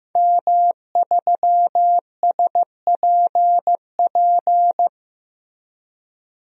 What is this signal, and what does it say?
Morse code, heard as a keyed tone, about 15 wpm, 700 Hz. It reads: M3SPP
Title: DXCC-100 Call Signs - 5 Characters - 15wpm